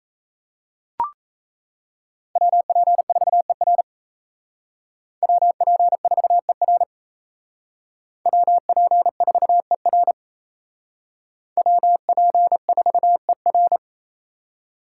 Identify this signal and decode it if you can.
Morse code, heard as a keyed tone, about 35 wpm, 700 Hz. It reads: WP4ER WP4ER WP4ER WP4ER